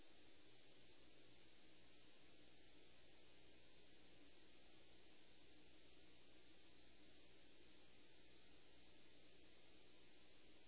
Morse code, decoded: none

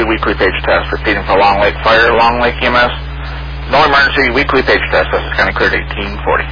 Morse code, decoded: none

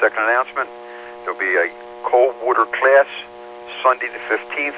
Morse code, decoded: none

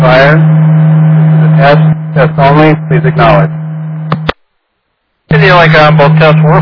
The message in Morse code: none

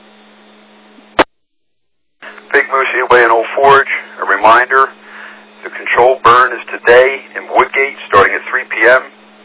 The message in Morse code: none

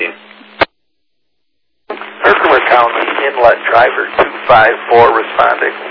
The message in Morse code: none

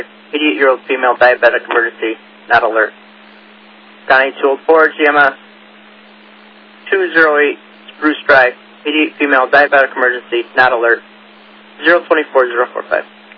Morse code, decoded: none

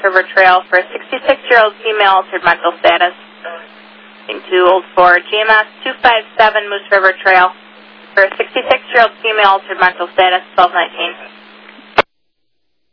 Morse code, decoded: none

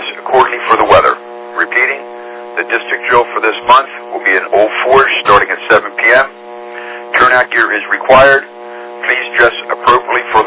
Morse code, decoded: none

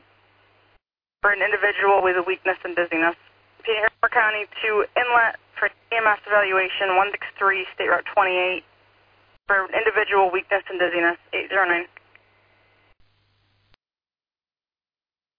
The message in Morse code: none

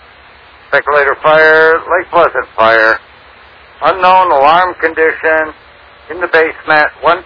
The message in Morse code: none

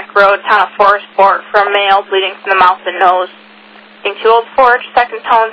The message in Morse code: none

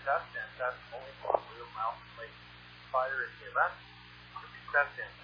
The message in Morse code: none

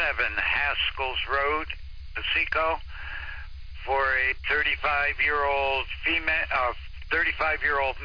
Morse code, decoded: none

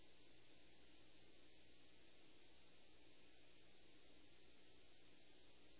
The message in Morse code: none

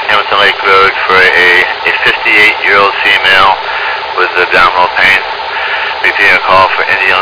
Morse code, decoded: none

